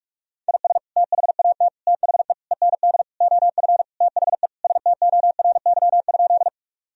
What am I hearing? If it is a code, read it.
Morse code, heard as a keyed tone, about 47 wpm, 700 Hz. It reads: IS THAT THE END OF THE STORY?
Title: Sentences from Top 1000 Words - Review Part 1 - 45wpm